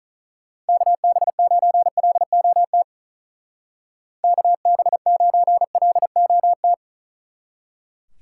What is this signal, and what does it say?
Morse code, heard as a keyed tone, about 45 wpm, 700 Hz. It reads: KB9LOT KB9LOT